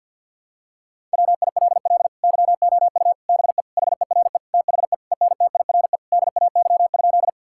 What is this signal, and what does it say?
Morse code, heard as a keyed tone, about 50 wpm, 700 Hz. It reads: WILL YOU BE HERE THE ENTIRE DAY?
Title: Sentences from Top 1000 Words - Review Part 5 - 50wpm